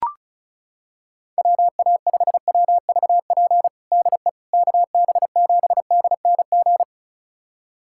Morse code, heard as a keyed tone, about 34 wpm, 700 Hz. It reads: WA5WVP DE KB7DNG
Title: USA Callsign DE Callsign - 6 Characters - 35wpm